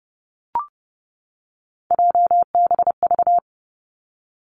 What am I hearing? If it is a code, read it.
Morse code, heard as a keyed tone, about 30 wpm, 700 Hz. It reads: JBV
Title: Three Challenging Letter-Number Combinations 30wpm